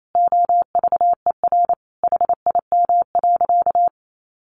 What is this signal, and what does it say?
Morse code, heard as a keyed tone, about 28 wpm, 700 Hz. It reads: OVER HIM.
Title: Sentences from Top 1100 Words 28wpm